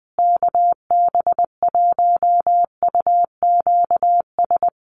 Morse code, heard as a keyed tone, about 20 wpm, 700 Hz. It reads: KB1UQS
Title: USA Call Signs Repeated 3x 20wpm